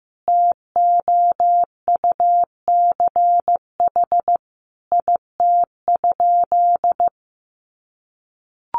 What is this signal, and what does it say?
Morse code, heard as a keyed tone, about 15 wpm, 700 Hz. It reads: TOUCH IT?
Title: Sentences from Top 1200 Words 15wpm